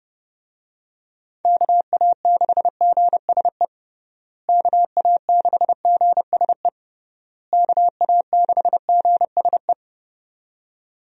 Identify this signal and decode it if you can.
Morse code, heard as a keyed tone, about 30 wpm, 700 Hz. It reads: KA6GSE KA6GSE KA6GSE
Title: USA Call Signs - 6 Characters - Repeated 3x - 30wpm